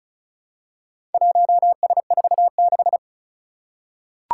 Morse code, heard as a keyed tone, about 36 wpm, 700 Hz. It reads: 1S46
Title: Four Challenging Letter-Number Combinations 35wpm